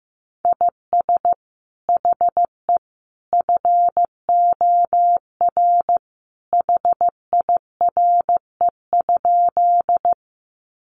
Morse code, heard as a keyed tone, about 15 wpm, 700 Hz. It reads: IS HE FOR HIRE?
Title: Sentences from Top 1400 Words 15wpm